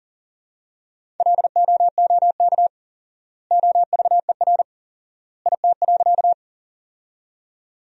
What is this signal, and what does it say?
Morse code, heard as a keyed tone, about 40 wpm, 700 Hz. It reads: LOOK OVER IT.
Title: 4x Word-Spacing - Sentences from Top 100 Words - 40wpm